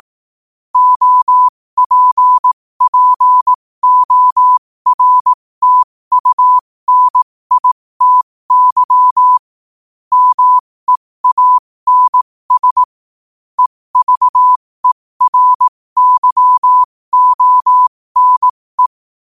Morse code, heard as a keyed tone, about 18 wpm, spaced as for 15 wpm, 1000 Hz. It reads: OPPORTUNITY MEANS EVERYONE
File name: 15wpm